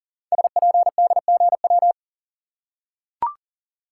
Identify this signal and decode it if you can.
Morse code, heard as a keyed tone, about 43 wpm, 690 Hz. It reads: SPDGW
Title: Code Groups - 5 Letters - 40wpm